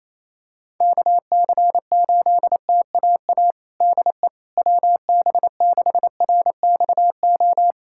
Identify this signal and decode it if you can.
Morse code, heard as a keyed tone, about 28 wpm, 700 Hz. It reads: KC8TAA DE WB6RXO
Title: USA Callsign DE Callsign - 6 Characters - 28wpm